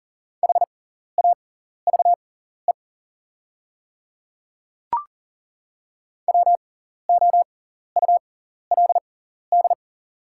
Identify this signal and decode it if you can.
Morse code, heard as a keyed tone, about 40 wpm, spaced as for 14 wpm, 700 Hz. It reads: HAVE WOULD
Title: 6x Character Spacing - Top 100 Words - 40wpm